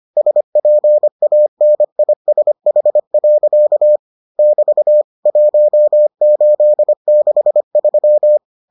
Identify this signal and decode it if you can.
Morse code, heard as a keyed tone, about 25 wpm, 600 Hz. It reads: SPANISH. <BT> 1863